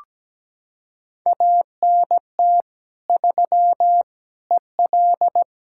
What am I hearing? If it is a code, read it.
Morse code, heard as a keyed tone, about 17 wpm, 700 Hz. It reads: ANT 3 EL
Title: Practice QSO - Exchange 2 - Info - 17wpm